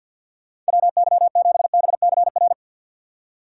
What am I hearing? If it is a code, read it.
Morse code, heard as a keyed tone, about 63 wpm, 700 Hz. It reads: WY7BCR